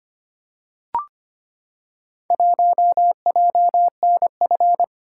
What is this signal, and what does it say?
Morse code, heard as a keyed tone, about 25 wpm, 700 Hz. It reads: 1JNF
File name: practice / 25wpm / Four Letter-Number Combinations - 25wpm